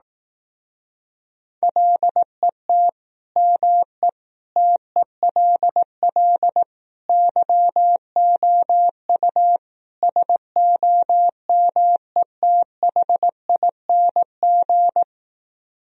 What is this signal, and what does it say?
Morse code, heard as a keyed tone, about 18 wpm, 700 Hz. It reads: LET ME TELL YOU SOMETHING
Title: Speed-Racing - Sets of 5 Words 15wpm